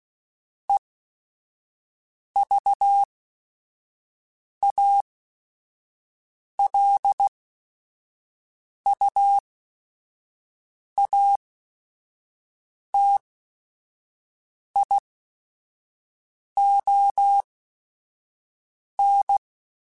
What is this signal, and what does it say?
Morse code, heard as a keyed tone, about 16 wpm, spaced as for 5 wpm, 800 Hz. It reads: EVALUATION
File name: Technical Subjects / Technical Subjects-mp3s / AND Gate @5